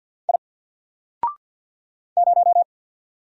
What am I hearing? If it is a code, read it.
Morse code, heard as a keyed tone, about 50 wpm, 700 Hz. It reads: I 0
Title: Single Letter-Number 50wpm